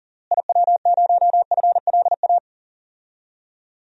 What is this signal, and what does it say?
Morse code, heard as a keyed tone, about 40 wpm, 700 Hz. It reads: IW0FLA
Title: DXCC-100 Call Signs - 6 Characters - Repeated 3x - 40wpm